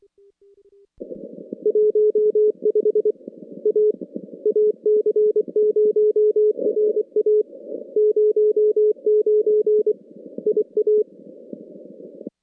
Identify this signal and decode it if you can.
Morse code, heard as a keyed tone, about 24 wpm, 420 Hz. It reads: 15 A AC0RA 09 IA
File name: contests / CW-SS-k5tr- -2200